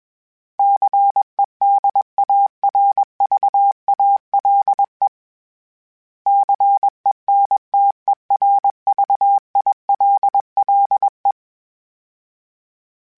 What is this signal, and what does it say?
Morse code, heard as a keyed tone, about 21 wpm, 800 Hz. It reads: CEDARVALE CENTERVILLE
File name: fromSMs / FromSMs mp3s / Cities@20